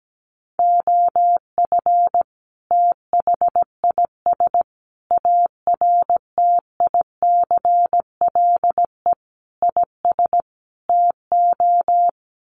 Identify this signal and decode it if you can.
Morse code, heard as a keyed tone, about 17 wpm, 700 Hz. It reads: OF THIS ARTICLE IS TO
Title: Sets of 5 Words Encore2 17wpm